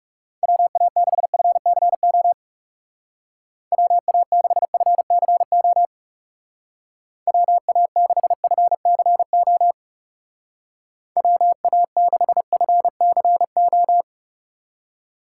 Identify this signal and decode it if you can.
Morse code, heard as a keyed tone, about 37 wpm, 700 Hz. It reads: WA6FCO WA6FCO WA6FCO WA6FCO
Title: Speed Racing - USA 2x3 Call Signs 30wpm